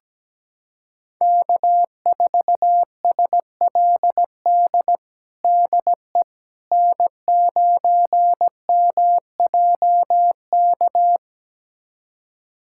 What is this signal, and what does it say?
Morse code, heard as a keyed tone, about 17 wpm, 700 Hz. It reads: K4SLD DE N9MJK
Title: USA Callsign DE Callsign - 5 Characters - 17wpm